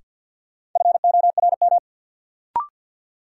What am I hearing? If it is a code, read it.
Morse code, heard as a keyed tone, about 53 wpm, 700 Hz. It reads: FORM
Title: Sentences from Top 1100 Words - New Word Drill Alt - 50wpm